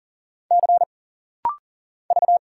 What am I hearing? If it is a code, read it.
Morse code, heard as a keyed tone, about 40 wpm, 700 Hz. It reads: C V
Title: Single Letter-Number Alt - Rapid-Fire 40wpm